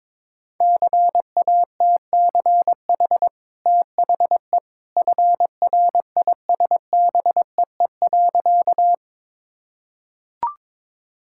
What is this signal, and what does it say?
Morse code, heard as a keyed tone, about 22 wpm, 690 Hz. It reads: CATCH THE FRISBEE.
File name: practice / 22wpm / Easy Sentences with Hints - 22wpm